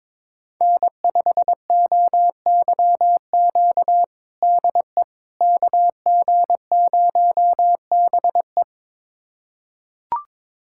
Morse code, heard as a keyed tone, about 22 wpm, 700 Hz. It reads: N5OYQ DE KG0BE
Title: USA Callsign DE Callsign 22wpm